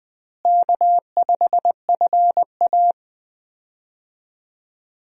Morse code, heard as a keyed tone, about 20 wpm, 700 Hz. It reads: K5FA
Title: USA Call Signs - 4 Characters - Repeated 2x - 20wpm